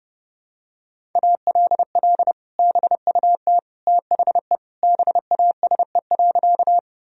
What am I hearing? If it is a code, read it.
Morse code, heard as a keyed tone, about 30 wpm, 700 Hz. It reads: ALL BUT THE BASE.
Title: Sentences from Top 1000 Words - Review Part 1 - 30wpm